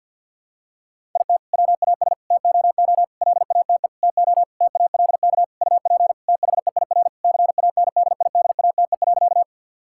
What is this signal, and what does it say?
Morse code, heard as a keyed tone, about 51 wpm, 700 Hz. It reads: IT WAS TOO LATE TO TALK UP THEIR CANDIDATE.
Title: Sentences from Top 1600 Words 50wpm